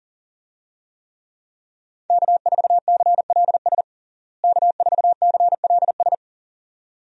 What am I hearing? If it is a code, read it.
Morse code, heard as a keyed tone, about 40 wpm, 700 Hz. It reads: K4CLS K4CLS
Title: USA Call Signs - 5 Characters - Repeated 2x - 40wpm